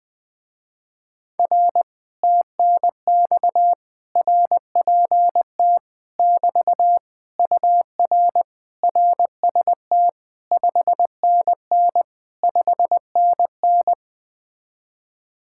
Morse code, heard as a keyed tone, about 20 wpm, 700 Hz. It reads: R TNX RPT <BT> UR RST 5NN 5NN
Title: Speed-Racing Common QSO Phrases - 17wpm